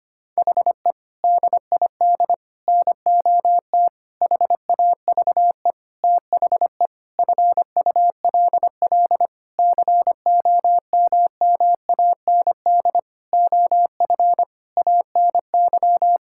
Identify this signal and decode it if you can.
Morse code, heard as a keyed tone, about 25 wpm, 700 Hz. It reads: HE DID NOT HAVE THE FULL COMMAND OF ANY
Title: Sentences from Top 1600 Words 25wpm